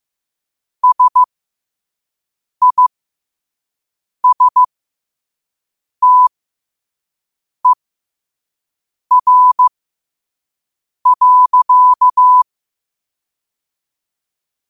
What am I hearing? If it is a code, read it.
Morse code, heard as a keyed tone, about 15 wpm, spaced as for 5 wpm, 1000 Hz. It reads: SISTER.
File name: qotdfiles / 5wpm